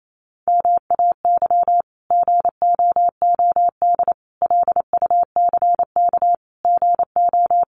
Morse code, heard as a keyed tone, about 28 wpm, 700 Hz. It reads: MAY GOOD LUCK GO
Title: Sentences from Top 1500 Words 28wpm